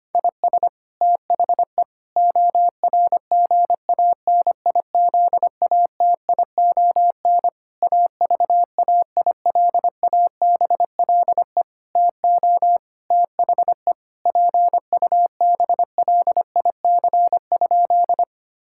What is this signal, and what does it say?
Morse code, heard as a keyed tone, about 25 wpm, 700 Hz. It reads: IS THE ORGANIZATION AVAILABLE TO THE PUBLIC?
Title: Sentences from Top 1000 Words - Review Part 5 - 25wpm